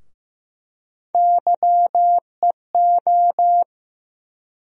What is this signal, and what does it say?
Morse code, heard as a keyed tone, about 15 wpm, 700 Hz. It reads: YEO
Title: USA Call Sign Postfixes - 3 Letters 15wpm